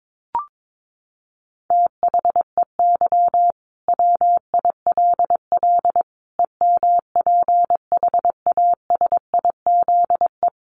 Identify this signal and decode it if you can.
Morse code, heard as a keyed tone, about 22 wpm, 700 Hz. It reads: THEY WILL EMPHASIZE
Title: Sentences from Top 1500 Words 22wpm